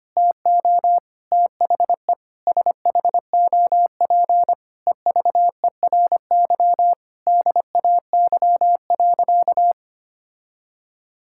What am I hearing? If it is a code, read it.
Morse code, heard as a keyed tone, about 25 wpm, 700 Hz. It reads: TO THE SHOP EVERY DAY.
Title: Sentences from Top 1600 Words 25wpm